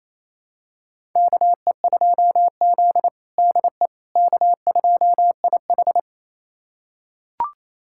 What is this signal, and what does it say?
Morse code, heard as a keyed tone, about 28 wpm, 700 Hz. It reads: KE2Z DE K2IH
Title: USA Callsign DE Callsign - 4 Characters - 28wpm